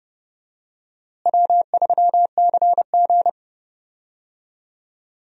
Morse code, heard as a keyed tone, about 32 wpm, 700 Hz. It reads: W3CG
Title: USA Call Signs - 4 Characters - Repeated 3x - 30wpm